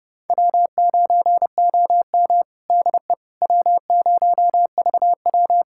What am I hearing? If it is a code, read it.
Morse code, heard as a keyed tone, about 32 wpm, 700 Hz. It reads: W9OM DE W0VW
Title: USA Callsign DE Callsign - 4 Characters - 30wpm